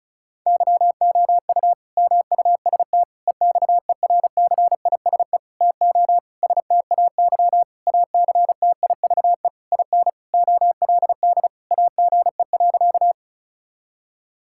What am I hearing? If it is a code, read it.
Morse code, heard as a keyed tone, about 34 wpm, 700 Hz. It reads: YOU MUST EXERCISE TO STAY ACTIVE IN OLD AGE.
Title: Sentences from Top 1000 Words - Review Part 6 - 35wpm